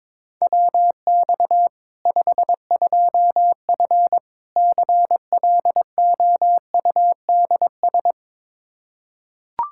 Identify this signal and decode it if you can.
Morse code, heard as a keyed tone, about 22 wpm, 700 Hz. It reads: WX 52F CLOUDS
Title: K1USN SST QSO Practice 22wpm